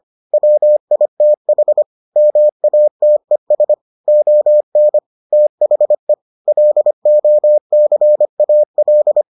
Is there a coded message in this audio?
Yes: Morse code from 0.3 to 9.2 s, heard as a keyed tone, about 25 wpm, 600 Hz. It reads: WITH MATES ON THE LOCAL